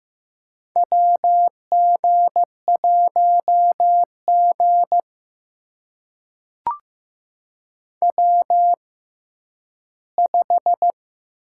Morse code, heard as a keyed tone, about 15 wpm, spaced as for 10 wpm, 700 Hz. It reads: WG1G W5
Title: USA Call Signs - 4 Characters - 6x Character Spacing - 15wpm